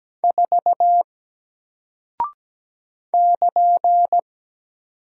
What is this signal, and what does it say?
Morse code, heard as a keyed tone, about 17 wpm, 700 Hz. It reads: 4 <KN>
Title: Single Character (Letter:Number:Punctuation:Prosign) Alt 17wpm